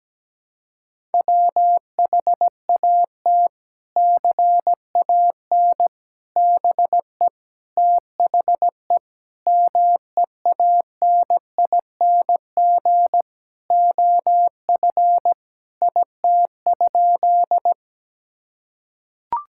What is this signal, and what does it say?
Morse code, heard as a keyed tone, about 17 wpm, 700 Hz. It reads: WHAT CAN BE THE MEANING OF IT?
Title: Sentences from Top 1000 Words - Review Part 6 - 17wpm